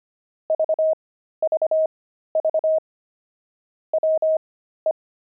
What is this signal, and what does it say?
Morse code, heard as a keyed tone, about 25 wpm, spaced as for 13 wpm, 630 Hz. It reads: VVV WE
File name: pangram-fs-13-cs-25-nine